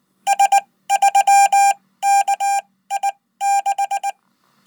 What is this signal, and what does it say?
Morse code, heard as a keyed tone, about 20 wpm, spaced as for 15 wpm, 800 Hz. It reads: S3KI6
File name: Code